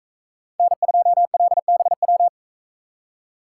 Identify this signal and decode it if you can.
Morse code, heard as a keyed tone, about 44 wpm, 700 Hz. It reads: N2LBW